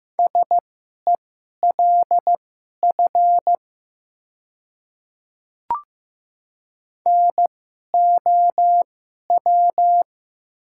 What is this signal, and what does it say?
Morse code, heard as a keyed tone, about 15 wpm, spaced as for 11 wpm, 700 Hz. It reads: SELF NOW
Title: 2x Character Spacing - Top 300 Words - 15wpm